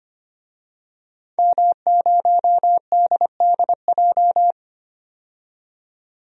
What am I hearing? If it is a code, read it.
Morse code, heard as a keyed tone, about 25 wpm, 700 Hz. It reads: M0DDJ
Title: DXCC-100 Call Signs - 5 Characters - 25wpm